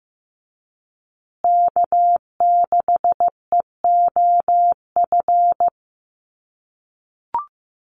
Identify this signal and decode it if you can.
Morse code, heard as a keyed tone, about 15 wpm, 700 Hz. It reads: K6EOF
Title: USA Call Signs Repeated 3x 15wpm